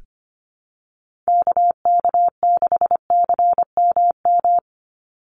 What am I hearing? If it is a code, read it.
Morse code, heard as a keyed tone, about 25 wpm, 700 Hz. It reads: KK6CMM
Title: USA Call Signs 25wpm